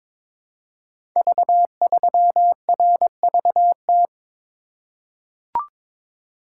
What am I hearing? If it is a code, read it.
Morse code, heard as a keyed tone, about 22 wpm, 700 Hz. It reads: V3RVT
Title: Code Groups - 5 Letter:Numbers - 22wpm